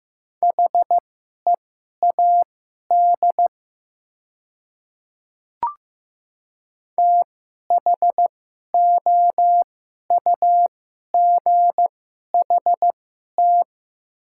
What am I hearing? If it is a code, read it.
Morse code, heard as a keyed tone, about 15 wpm, spaced as for 11 wpm, 690 Hz. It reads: HEAD THOUGHT
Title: 2x Character Spacing - Top 300 Words - 15wpm